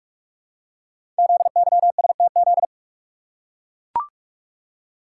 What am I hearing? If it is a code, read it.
Morse code, heard as a keyed tone, about 44 wpm, 700 Hz. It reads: ZYSTZ
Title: Code Groups - 5 Letters - 45wpm